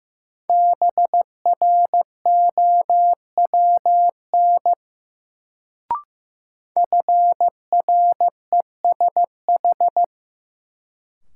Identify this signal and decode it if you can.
Morse code, heard as a keyed tone, about 15 wpm, 700 Hz. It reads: BROWN FRESH
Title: Sentences from Top 1200 Words - New Word Drill - 15wpm